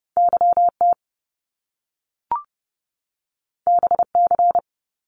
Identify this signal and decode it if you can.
Morse code, heard as a keyed tone, about 30 wpm, 700 Hz. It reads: YT BC
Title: Canadian Provinces and Territories 30wpm